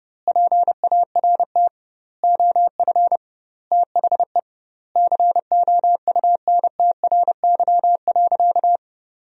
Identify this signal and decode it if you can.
Morse code, heard as a keyed tone, about 30 wpm, 700 Hz. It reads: PART OF THE COUNTRY.
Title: 2x Word-Spacing - Sentences from Top 300 Words - 30wpm